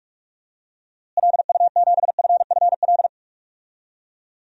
Morse code, heard as a keyed tone, about 45 wpm, 700 Hz. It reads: LU7FFL
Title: DXCC-100 Call Signs - 6 Characters - Repeated 3x - 45wpm